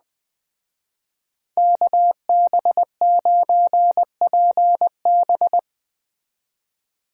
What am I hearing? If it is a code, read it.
Morse code, heard as a keyed tone, about 21 wpm, 700 Hz. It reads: KB9PB
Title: Speed Racing - USA 2x2 Call Signs 17wpm